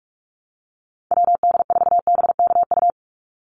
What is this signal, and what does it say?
Morse code, heard as a keyed tone, about 46 wpm, 690 Hz. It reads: WD4BKU
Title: USA Call Signs 45wpm